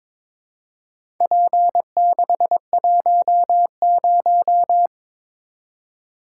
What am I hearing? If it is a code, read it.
Morse code, heard as a keyed tone, about 22 wpm, 700 Hz. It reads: P610
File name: practice / 22wpm / Four Letter-Number Combinations - 22wpm